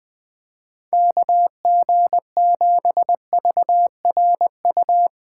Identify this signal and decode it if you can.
Morse code, heard as a keyed tone, about 21 wpm, 700 Hz. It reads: KG7VRU
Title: Speed Racing - USA 2x3 Call Signs 17wpm